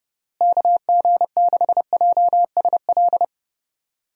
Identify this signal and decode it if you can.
Morse code, heard as a keyed tone, about 31 wpm, 700 Hz. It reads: KG6JSL
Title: USA Call Signs - 6 Characters - Repeated 3x - 30wpm